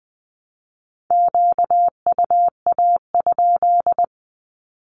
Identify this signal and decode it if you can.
Morse code, heard as a keyed tone, about 20 wpm, 700 Hz. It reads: QUA?
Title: Q-Codes with Spoken Meaning - 20wpm